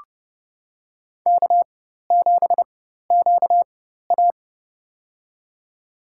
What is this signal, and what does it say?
Morse code, heard as a keyed tone, about 29 wpm, spaced as for 14 wpm, 700 Hz. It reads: K7QA
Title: USA Call Signs - 4 Characters - 4x Character Spacing - 30wpm